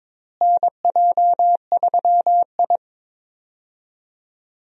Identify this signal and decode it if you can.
Morse code, heard as a keyed tone, about 22 wpm, 700 Hz. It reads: NJ3I
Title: USA Call Signs - 4 Characters - Repeated 2x - 22wpm